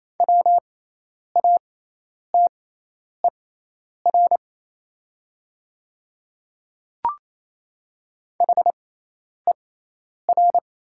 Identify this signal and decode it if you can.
Morse code, heard as a keyed tone, about 27 wpm, spaced as for 10 wpm, 690 Hz. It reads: WATER HER
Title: 6x Character Spacing - Top 100 Words - 28wpm